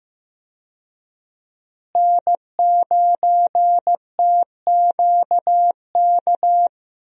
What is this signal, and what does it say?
Morse code, heard as a keyed tone, about 15 wpm, 700 Hz. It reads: N9TQK
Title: USA Call Signs - 5 Characters - Repeated 3x - 15wpm